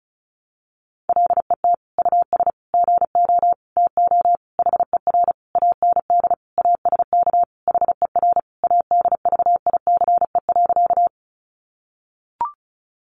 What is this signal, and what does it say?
Morse code, heard as a keyed tone, about 34 wpm, 700 Hz. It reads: LET US GO TO HER AND ASK HER ADVICE.
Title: Sentences from Top 1100 Words 35wpm